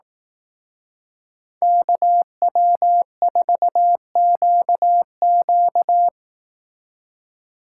Morse code, heard as a keyed tone, about 18 wpm, 700 Hz. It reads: KW4QQ
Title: Speed Racing - USA Call Signs 15wpm